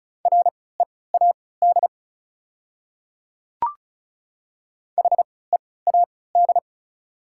Morse code, heard as a keyed tone, about 34 wpm, spaced as for 20 wpm, 700 Hz. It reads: READ HEAD
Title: 3x Character Spacing - Top 300 Words - 35wpm